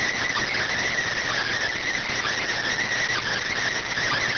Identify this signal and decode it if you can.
Emergency Alert System (SAME) tones, heard as harsh buzzing bursts. It is heard from 0.5 to 3.5 s.